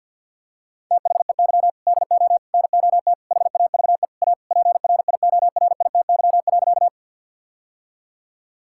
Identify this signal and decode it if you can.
Morse code, heard as a keyed tone, about 50 wpm, 700 Hz. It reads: THEY DO NOT HAVE A PRIORITY.
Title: Sentences from Top 1600 Words 50wpm